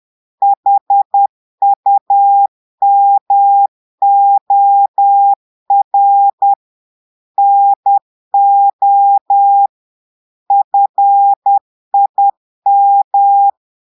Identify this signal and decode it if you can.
Morse code, heard as a keyed tone, about 10 wpm, 800 Hz. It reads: HUMOR NO FIM
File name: portugais texte en clair 2 vitesse 10 mots minutes